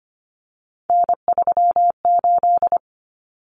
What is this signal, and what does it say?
Morse code, heard as a keyed tone, about 25 wpm, 700 Hz. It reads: N38
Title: Common Cut Numbers - 3-Digit 25wpm